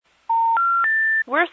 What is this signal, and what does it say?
Special information tone, with speech.